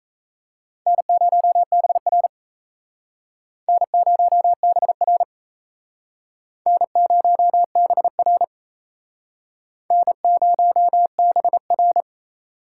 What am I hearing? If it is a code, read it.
Morse code, heard as a keyed tone, about 36 wpm, 700 Hz. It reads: N0BR N0BR N0BR N0BR